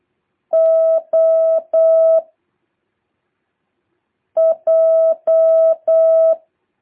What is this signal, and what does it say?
Morse code, heard as a keyed tone, about 8 wpm, 660 Hz. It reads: O J
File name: jh8cbh / 2023cw